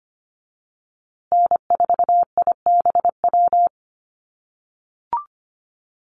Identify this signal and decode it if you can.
Morse code, heard as a keyed tone, about 25 wpm, 700 Hz. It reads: N4IBW
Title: USA Call Signs Repeated 3x 25wpm